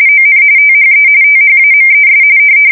RTTY, about 45.45 baud. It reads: ON THE THAMES
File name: fsk170 / battleshipnj